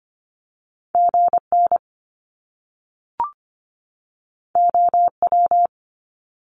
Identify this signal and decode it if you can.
Morse code, heard as a keyed tone, about 25 wpm, 700 Hz. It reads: GN OW
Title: Top Two Letter Combinations in Words 25wpm